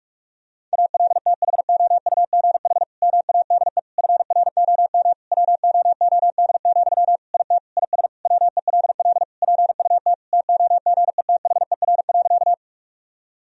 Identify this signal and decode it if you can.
Morse code, heard as a keyed tone, about 41 wpm, 700 Hz. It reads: ALTHOUGH MADE FROM WOOD, IT IS WELL PUT TOGETHER.
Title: Sentences from Top 1000 Words - Review Part 3 - 45wpm